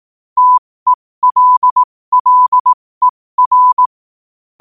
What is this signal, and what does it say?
Morse code, heard as a keyed tone, about 18 wpm, spaced as for 15 wpm, 1000 Hz. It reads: TELLER
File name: qotdfiles / 15wpm